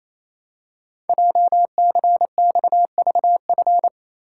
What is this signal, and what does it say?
Morse code, heard as a keyed tone, about 27 wpm, 700 Hz. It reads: JCXVF
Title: Code Groups - 5 Letters - 28wpm